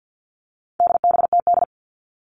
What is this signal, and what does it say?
Morse code, heard as a keyed tone, about 50 wpm, 700 Hz. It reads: DBTD
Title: All Cut Numbers - 4-Digit 50wpm